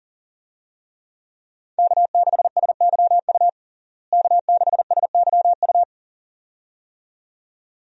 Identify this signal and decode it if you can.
Morse code, heard as a keyed tone, about 40 wpm, 700 Hz. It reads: K6SYU K6SYU
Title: USA Call Signs - 5 Characters - Repeated 2x - 40wpm